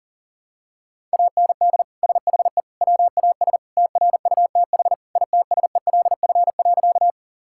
Morse code, heard as a keyed tone, about 37 wpm, 700 Hz. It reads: AND SHE WAS TRUTH ITSELF.
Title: Sentences from Top 1000 Words - Review Part 5 - 40wpm